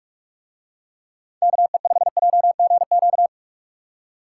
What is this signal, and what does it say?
Morse code, heard as a keyed tone, about 44 wpm, 700 Hz. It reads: KE5JGQ